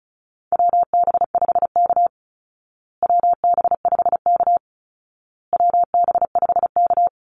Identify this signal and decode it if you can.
Morse code, heard as a keyed tone, about 34 wpm, 700 Hz. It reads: WB5K WB5K WB5K
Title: Call Signs Repeated 2x - 35wpm